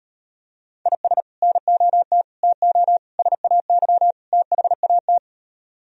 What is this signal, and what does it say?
Morse code, heard as a keyed tone, about 38 wpm, 700 Hz. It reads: IS NOT TO SAY THAT